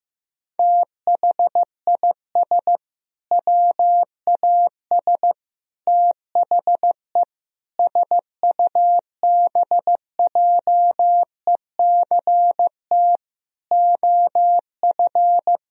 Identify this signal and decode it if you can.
Morse code, heard as a keyed tone, about 15 wpm, 700 Hz. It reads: THIS WAS THE SUBJECT OF
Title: Sentences from Top 1000 Words - Review Part 1 - 15wpm